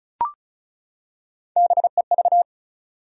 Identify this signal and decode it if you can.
Morse code, heard as a keyed tone, about 35 wpm, 700 Hz. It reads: BEV
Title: USA Common Names - Max 5 Letters - 35wpm